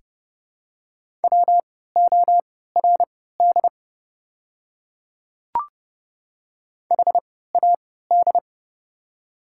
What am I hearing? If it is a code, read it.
Morse code, heard as a keyed tone, about 30 wpm, spaced as for 17 wpm, 700 Hz. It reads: WORD HAD
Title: 3x Character Spacing - Top 200 Words - 30wpm